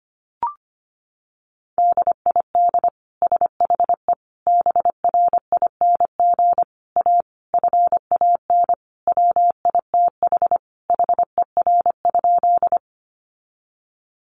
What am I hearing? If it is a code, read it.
Morse code, heard as a keyed tone, about 25 wpm, 700 Hz. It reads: DID SHE BRING A FAN WITH HER?
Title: Sentences from Top 1300 Words 25wpm